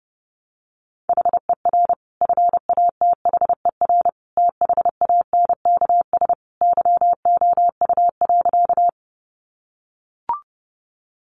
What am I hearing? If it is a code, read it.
Morse code, heard as a keyed tone, about 30 wpm, 700 Hz. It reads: HER FATHER THANKS YOU.
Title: Sentences from Top 600 Words 30wpm